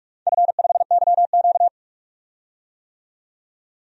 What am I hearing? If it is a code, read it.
Morse code, heard as a keyed tone, about 47 wpm, 700 Hz. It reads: F5YQ
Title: DXCC-100 Call Signs - 4 Characters - 45wpm